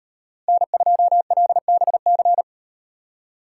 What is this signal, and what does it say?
Morse code, heard as a keyed tone, about 38 wpm, 700 Hz. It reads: N2LBC